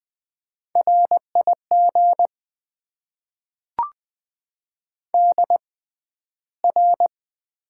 Morse code, heard as a keyed tone, about 20 wpm, spaced as for 12 wpm, 700 Hz. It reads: RIG DR
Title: Top 90 QSO Elements - 6x Character Spacing - 20wpm